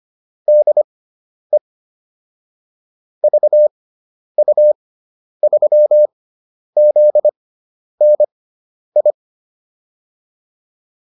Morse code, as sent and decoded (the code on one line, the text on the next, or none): -.. / ...- ..- ...-- --.. -. ..
DE VU3ZNI